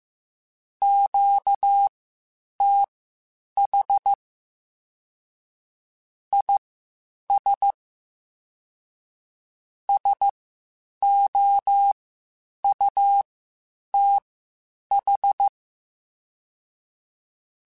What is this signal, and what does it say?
Morse code, heard as a keyed tone, about 15 wpm, spaced as for 8 wpm, 800 Hz. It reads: QTH IS SOUTH
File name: (spaced) / QSOs Spaced-mp3s / WA7QOX 2 @13 sp